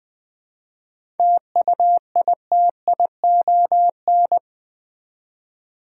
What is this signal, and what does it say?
Morse code, heard as a keyed tone, about 20 wpm, 700 Hz. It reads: TUITION
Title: 7 Letter Words - First 3 Letters Given - 20wpm